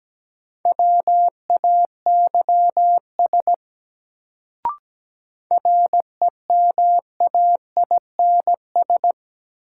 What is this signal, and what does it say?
Morse code, heard as a keyed tone, about 17 wpm, 700 Hz. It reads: WAYS REMAINS
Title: Sets of 5 Words Encore - New Word Drill - 17wpm